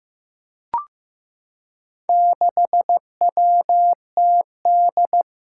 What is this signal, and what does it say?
Morse code, heard as a keyed tone, about 15 wpm, 700 Hz. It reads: 6WTD
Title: Four Challenging Letter-Number Combinations 15wpm